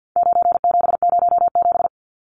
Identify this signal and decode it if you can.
Morse code, heard as a keyed tone, about 49 wpm, 700 Hz. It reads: 9707
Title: Numbers - 4-Digit 50wpm